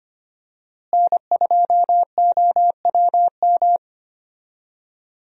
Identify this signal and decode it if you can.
Morse code, heard as a keyed tone, about 25 wpm, 700 Hz. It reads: N2OWM
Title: USA Call Signs - 5 Characters - Repeated 3x - 25wpm